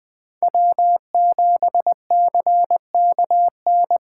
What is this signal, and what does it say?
Morse code, heard as a keyed tone, about 20 wpm, 700 Hz. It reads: W7CKN
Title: USA Call Signs - 5 Characters - 8x Character Spacing - 20wpm